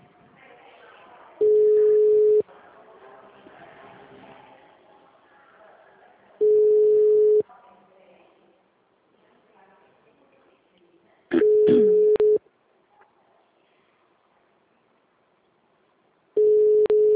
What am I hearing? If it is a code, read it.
Ringback tone (Europe, 425 Hz).